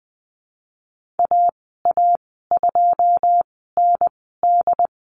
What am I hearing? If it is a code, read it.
Morse code, heard as a keyed tone, about 20 wpm, spaced as for 15 wpm, 700 Hz. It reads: AA2ND
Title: 128 Lesson 21g - Number 6 - Call Signs 20wpm